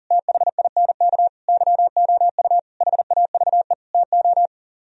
Morse code, heard as a keyed tone, about 37 wpm, 700 Hz. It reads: THINK YOU HAVE TO
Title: Speed-Racing - Sets of 5 Words 40wpm